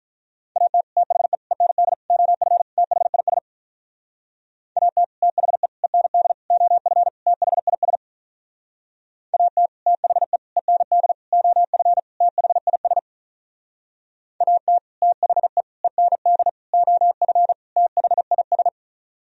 Morse code, heard as a keyed tone, about 47 wpm, 700 Hz. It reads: AT THE END OF THIS AT THE END OF THIS AT THE END OF THIS AT THE END OF THIS